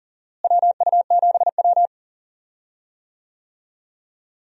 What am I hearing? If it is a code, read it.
Morse code, heard as a keyed tone, about 40 wpm, 700 Hz. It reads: WU7W
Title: USA Call Signs - 4 Characters - Repeated 2x - 40wpm